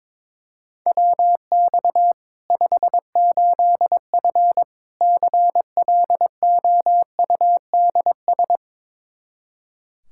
Morse code, heard as a keyed tone, about 21 wpm, 700 Hz. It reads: WX 58F CLOUDS
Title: Practice QSO - Exchange 2 - Info - 22wpm